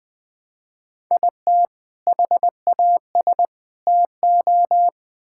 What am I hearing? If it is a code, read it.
Morse code, heard as a keyed tone, about 20 wpm, 700 Hz. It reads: IT HAS TO
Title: Build A Phrase - 4-Word Phrases - 20wpm